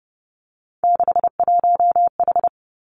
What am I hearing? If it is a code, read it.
Morse code, heard as a keyed tone, about 32 wpm, 700 Hz. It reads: 61H
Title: Three Challenging Letter-Number Combinations 30wpm